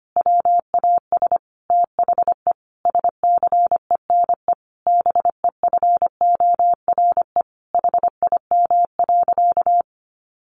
Morse code, heard as a keyed tone, about 25 wpm, 700 Hz. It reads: WAS THE SCENE BEFORE HIM.
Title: Sentences from Top 1100 Words 25wpm